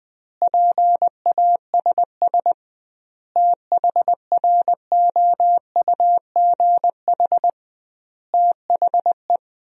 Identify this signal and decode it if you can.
Morse code, heard as a keyed tone, about 20 wpm, 700 Hz. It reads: PASS THROUGH THE
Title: 2x Word-Spacing - Sentences from Top 400 Words - 20wpm